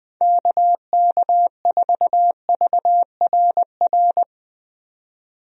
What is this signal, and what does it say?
Morse code, heard as a keyed tone, about 20 wpm, 700 Hz. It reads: KK4VRR
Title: USA Call Signs - 6 Characters - 20wpm